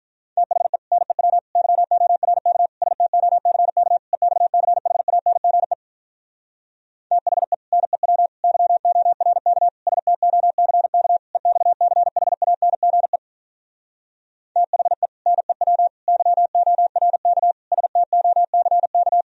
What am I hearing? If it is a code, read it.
Morse code, heard as a keyed tone, about 46 wpm, 700 Hz. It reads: THE NEW YORK STOCK EXCHANGE THE NEW YORK STOCK EXCHANGE THE NEW YORK STOCK